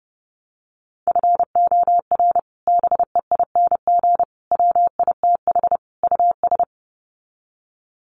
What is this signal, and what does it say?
Morse code, heard as a keyed tone, about 30 wpm, 700 Hz. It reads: FOR BEING WITH US
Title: Sets of 4 Words 30wpm